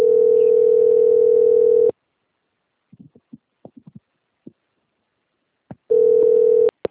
Ringback tone (North America), 440 + 480 Hz.